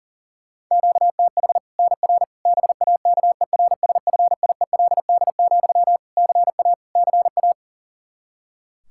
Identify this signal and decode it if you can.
Morse code, heard as a keyed tone, about 37 wpm, 700 Hz. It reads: QTH NR BAKERSFIELD, CA CA
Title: Practice QSO - Exchange 1 - Beginning - 40wpm